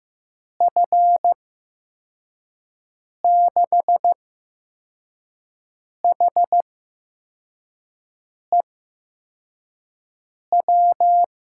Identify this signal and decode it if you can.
Morse code, heard as a keyed tone, about 15 wpm, spaced as for 4 wpm, 700 Hz. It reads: F6HEW
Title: USA Call Signs - 8x Character Spacing - 15wpm